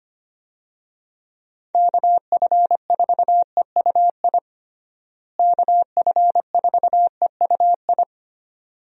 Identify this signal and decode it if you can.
Morse code, heard as a keyed tone, about 25 wpm, 700 Hz. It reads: KF4EUI KF4EUI
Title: USA Call Signs - 6 Characters - Repeated 2x - 25wpm